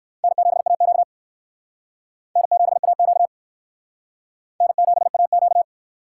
Morse code, heard as a keyed tone, about 56 wpm, 700 Hz. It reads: N7AQ N7AQ N7AQ
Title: Speed Racing - USA 1x2 Call Signs 45wpm